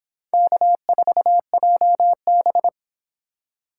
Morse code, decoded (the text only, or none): K4JB